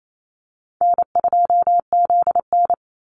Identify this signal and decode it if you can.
Morse code, heard as a keyed tone, about 28 wpm, 700 Hz. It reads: N2ZN
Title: Easy Call Signs - 28wpm